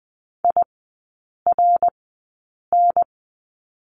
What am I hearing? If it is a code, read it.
Morse code, heard as a keyed tone, about 20 wpm, spaced as for 8 wpm, 700 Hz. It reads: IRN
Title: 067 Lesson 12e - Letter d - Three Characters 20wpm